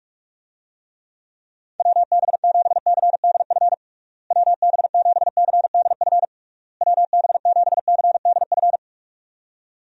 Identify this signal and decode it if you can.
Morse code, heard as a keyed tone, about 46 wpm, 700 Hz. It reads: WB7CDF WB7CDF WB7CDF
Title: USA Call Signs - 6 Characters - Repeated 3x - 45wpm